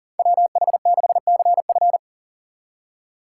Morse code, heard as a keyed tone, about 38 wpm, 700 Hz. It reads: WH6CF
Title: USA Call Signs - 2x Character Spacing - 40wpm